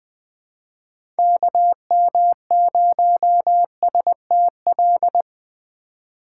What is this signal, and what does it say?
Morse code, heard as a keyed tone, about 20 wpm, 700 Hz. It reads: KM0STL
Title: USA Call Signs - 6 Characters - 4x Character Spacing - 20wpm